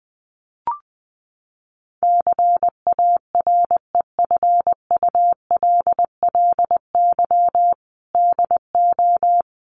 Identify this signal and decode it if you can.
Morse code, heard as a keyed tone, about 20 wpm, 700 Hz. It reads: CAREFULLY DO
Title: Sentences from Top 1300 Words 20wpm